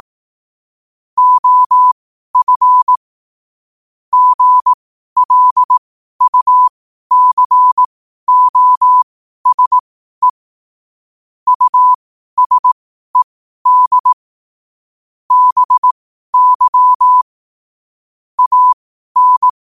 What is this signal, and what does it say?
Morse code, heard as a keyed tone, about 18 wpm, spaced as for 12 wpm, 1000 Hz. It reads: OF GLUCOSE USED BY AN
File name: qotdfiles / 12wpm